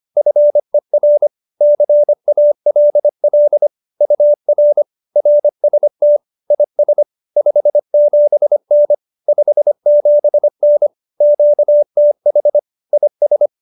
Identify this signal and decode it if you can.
Morse code, heard as a keyed tone, about 25 wpm, 590 Hz. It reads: FER CALL UR RST IS 57N 57N QTH IS
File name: qso201b 25